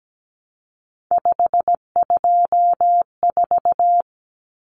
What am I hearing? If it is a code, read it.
Morse code, heard as a keyed tone, about 17 wpm, 700 Hz. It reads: 524
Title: Common Cut Numbers - 3-Digit 17wpm